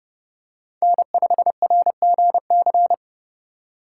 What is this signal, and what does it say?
Morse code, heard as a keyed tone, about 30 wpm, 700 Hz. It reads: N5RGC
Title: USA Call Signs - 5 Characters - 3x Character Spacing - 30wpm